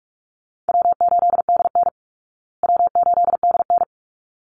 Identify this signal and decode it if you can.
Morse code, heard as a keyed tone, about 44 wpm, 700 Hz. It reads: W8DN W8DN
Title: Easy Call Signs Repeated 2x - 45wpm